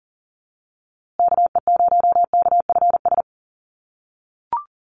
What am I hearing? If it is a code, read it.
Morse code, heard as a keyed tone, about 39 wpm, 700 Hz. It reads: KE0KFS
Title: USA Call Signs Repeated 2x 40wpm